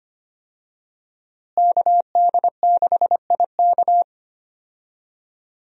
Morse code, heard as a keyed tone, about 25 wpm, 690 Hz. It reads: KD6IK
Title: USA Call Signs - 5 Characters - Repeated 2x - 25wpm